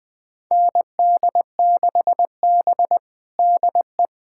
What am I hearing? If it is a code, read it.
Morse code, heard as a keyed tone, about 20 wpm, 700 Hz. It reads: ND6B DE
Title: USA Callsign DE Callsign 20wpm